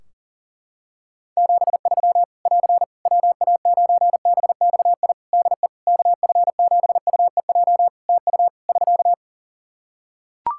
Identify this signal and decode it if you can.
Morse code, heard as a keyed tone, about 40 wpm, 700 Hz. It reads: 73 <AR> WA9BXI DE KF7UEJ TU <SK>
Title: K1USN SST QSO Practice 40wpm